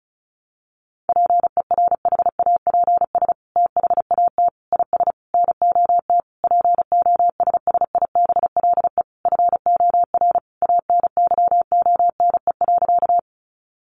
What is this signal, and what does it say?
Morse code, heard as a keyed tone, about 34 wpm, 700 Hz. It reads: PERHAPS THAT IS NOT POSSIBLE FOR ANYONE.
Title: Sentences from Top 900 Words 35wpm